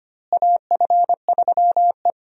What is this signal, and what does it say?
Morse code, heard as a keyed tone, about 25 wpm, 700 Hz. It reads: AF3E
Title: USA Call Signs - 4 Characters - Repeated 3x - 25wpm